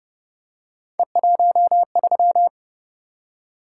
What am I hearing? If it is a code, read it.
Morse code, heard as a keyed tone, about 29 wpm, 700 Hz. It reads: E13